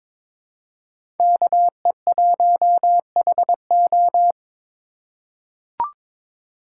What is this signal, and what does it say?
Morse code, heard as a keyed tone, about 22 wpm, 700 Hz. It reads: KE1HO
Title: USA Call Signs - 5 Characters - 4x Character Spacing - 22wpm